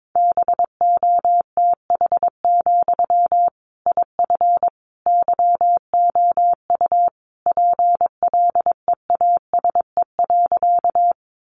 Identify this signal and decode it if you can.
Morse code, heard as a keyed tone, about 22 wpm, 700 Hz. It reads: BOTH, IF YOU PLEASE.
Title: Sentences from Top 1500 Words 22wpm